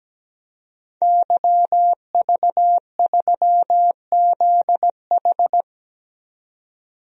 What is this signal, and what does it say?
Morse code, heard as a keyed tone, about 17 wpm, 700 Hz. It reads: YV3ZH
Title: DXCC-100 Call Signs - 5 Characters - 17wpm